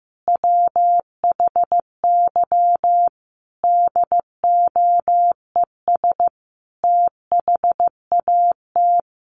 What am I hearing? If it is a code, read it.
Morse code, heard as a keyed tone, about 15 wpm, 700 Hz. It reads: WHY DOES THAT
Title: Sentences from Top 700 Words 15wpm